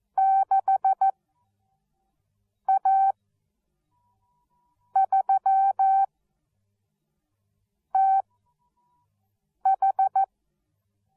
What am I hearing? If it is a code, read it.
Morse code, heard as a keyed tone, about 14 wpm, spaced as for 4 wpm, 770 Hz. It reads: 6A3TH